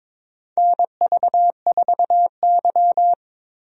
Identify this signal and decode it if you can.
Morse code, heard as a keyed tone, about 22 wpm, 700 Hz. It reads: NV4Y
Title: USA Call Signs - 4 Characters - Repeated 3x - 22wpm